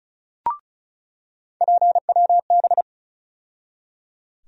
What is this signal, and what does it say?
Morse code, heard as a keyed tone, about 36 wpm, 700 Hz. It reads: PWB
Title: USA Call Sign Postfixes - 3 Letters 35wpm